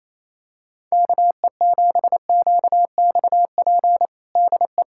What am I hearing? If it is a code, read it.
Morse code, heard as a keyed tone, about 28 wpm, 700 Hz. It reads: KE7QXP DE
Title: USA Callsign DE Callsign 28wpm